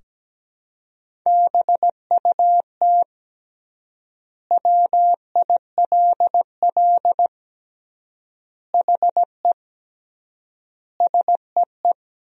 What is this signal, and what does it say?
Morse code, heard as a keyed tone, about 17 wpm, 700 Hz. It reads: BUT WILL HE SEE
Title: 3x Word-Spacing - Sentences from Top 100 Words - 17wpm